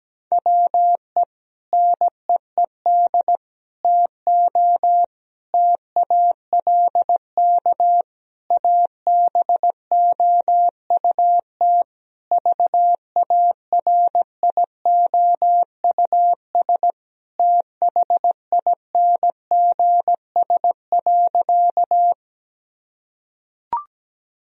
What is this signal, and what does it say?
Morse code, heard as a keyed tone, about 17 wpm, 700 Hz. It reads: WE NEED TO TALK ABOUT VARIOUS THINGS.
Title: Sentences from Top 1000 Words - Review Part 4 - 17wpm